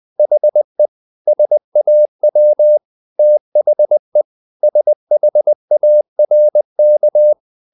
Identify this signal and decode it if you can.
Morse code, heard as a keyed tone, about 20 wpm, 600 Hz. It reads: HE SAW THE SHARK
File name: the old man and the sea part 2 20wpm